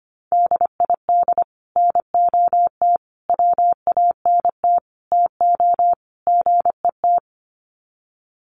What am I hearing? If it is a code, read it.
Morse code, heard as a keyed tone, about 25 wpm, 700 Hz. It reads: DID NOT WANT TO GET
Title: Sets of 5 Words Encore2 25wpm